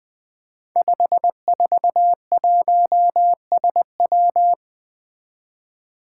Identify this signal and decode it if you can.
Morse code, heard as a keyed tone, about 20 wpm, 700 Hz. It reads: 541SW
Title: Five Challenging Letter-Number Combinations 20wpm